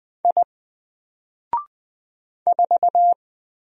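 Morse code, heard as a keyed tone, about 20 wpm, 700 Hz. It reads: I 4
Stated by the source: Single letter and number Morse drill at 20 wpm, from N4LCD.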